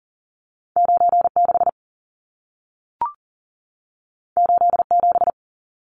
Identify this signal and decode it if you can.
Morse code, heard as a keyed tone, about 40 wpm, 700 Hz. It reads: 96 87
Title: Numbers - 2-Digit 40wpm